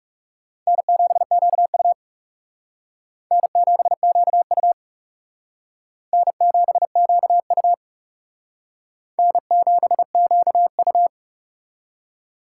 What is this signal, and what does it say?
Morse code, heard as a keyed tone, about 37 wpm, 700 Hz. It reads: N7QU N7QU N7QU N7QU
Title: Speed Racing - USA 1x2 Call Signs 30wpm